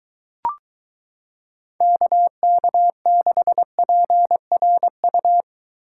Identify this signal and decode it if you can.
Morse code, heard as a keyed tone, about 22 wpm, 700 Hz. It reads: KK6PRU